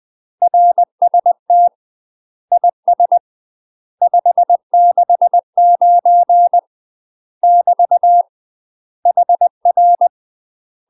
Morse code, heard as a keyed tone, about 20 wpm, 700 Hz. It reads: RST IS 569 <BT> HR
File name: UK MASTER QSOs TRAINING / 20WPM / UK MASTER QSOs 0018